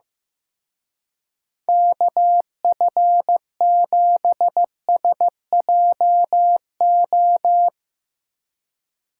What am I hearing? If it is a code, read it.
Morse code, heard as a keyed tone, about 16 wpm, 700 Hz. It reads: KF7SJO
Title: Speed Racing - USA 2x3 Call Signs 15wpm